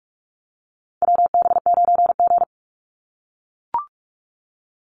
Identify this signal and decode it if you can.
Morse code, heard as a keyed tone, about 44 wpm, 700 Hz. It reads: WB9G
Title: Call Signs - 45wpm